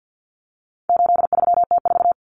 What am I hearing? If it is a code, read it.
Morse code, heard as a keyed tone, about 50 wpm, 700 Hz. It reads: 83T4
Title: Common Cut Numbers - 4-Digit 50wpm